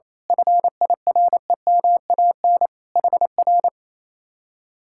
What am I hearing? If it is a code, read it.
Morse code, heard as a keyed tone, about 27 wpm, 700 Hz. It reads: FIREMAN HR
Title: Speed-Racing Common QSO Phrases Encore - 28wpm